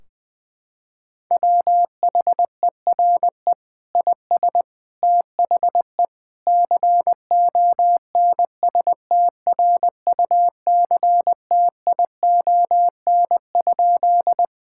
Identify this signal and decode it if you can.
Morse code, heard as a keyed tone, about 20 wpm, 700 Hz. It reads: WHERE IS THE CONSTRUCTION?
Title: Sentences from Top 1600 Words 20wpm